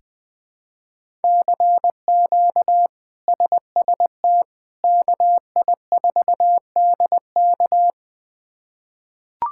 Morse code, heard as a keyed tone, about 20 wpm, 700 Hz. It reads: CQ SST KI4DK
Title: K1USN SST QSO Practice - 20wpm